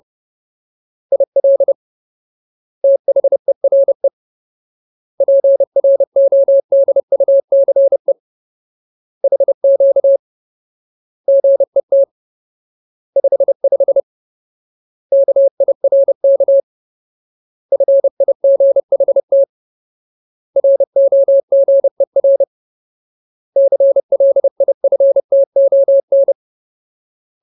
Morse code, heard as a keyed tone, about 30 wpm, 550 Hz. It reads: IL THERE PRODUCE HQ GET 55 KIRK FIGHT ROGER CLIFTON
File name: Ultimate Word List / 30WPM Ultimate